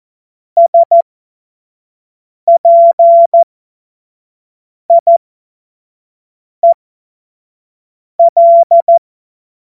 Morse code, heard as a keyed tone, about 14 wpm, spaced as for 5 wpm, 680 Hz. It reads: SPIEL
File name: Morsekurs / Witze / Witz 09. Februar 2021, 25 F70